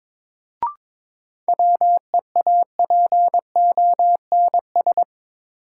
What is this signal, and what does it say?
Morse code, heard as a keyed tone, about 22 wpm, 700 Hz. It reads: WEAPONS
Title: Sets of 5 Words Encore - New Word Drill - 22wpm